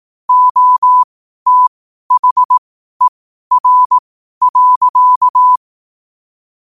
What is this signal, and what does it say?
Morse code, heard as a keyed tone, about 18 wpm, spaced as for 12 wpm, 1000 Hz. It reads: OTHER.
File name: qotdfiles / 12wpm